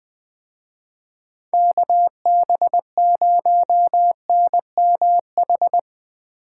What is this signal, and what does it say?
Morse code, heard as a keyed tone, about 23 wpm, 700 Hz. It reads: KB0NMH
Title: Speed Racing - USA 2x3 Call Signs 20wpm